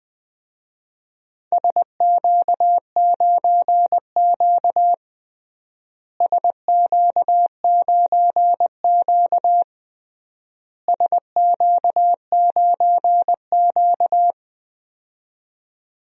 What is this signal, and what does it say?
Morse code, heard as a keyed tone, about 20 wpm, 700 Hz. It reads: SQ9Q SQ9Q SQ9Q
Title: DXCC-100 Call Signs - 4 Characters - Repeated 3x - 20wpm